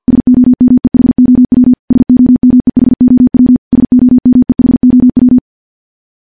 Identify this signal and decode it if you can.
Morse code, heard as a keyed tone, about 50 wpm, 250 Hz. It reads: SOMEHOW SOMEHOW SOMEHOW
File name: morse / morse-training-50WPM